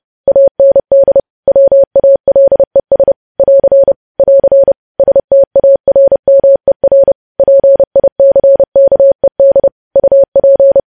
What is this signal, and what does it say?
Morse code, heard as a keyed tone, about 30 wpm, 550 Hz. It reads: AND WALES <AR> <AR> STARMER PICKED UP
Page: news headlines in morse code at 30 wpm